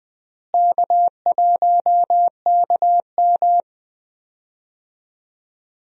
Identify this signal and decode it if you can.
Morse code, heard as a keyed tone, about 20 wpm, 700 Hz. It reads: K1KM
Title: USA Call Signs - 4 Characters - Repeated 2x - 20wpm